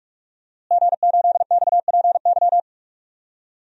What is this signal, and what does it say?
Morse code, heard as a keyed tone, about 47 wpm, 700 Hz. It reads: G8XPY
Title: DXCC-100 Call Signs - 5 Characters - Repeated 3x - 45wpm